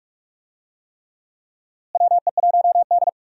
Morse code, heard as a keyed tone, about 46 wpm, 700 Hz. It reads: WE1D
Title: USA Call Signs - 4 Characters - Repeated 3x - 45wpm